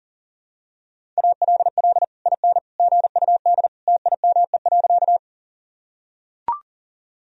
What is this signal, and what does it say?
Morse code, heard as a keyed tone, about 40 wpm, 700 Hz. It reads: ALL IN GUD TIME.
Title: Sentences from Top 200 Words and Abbreviations 40wpm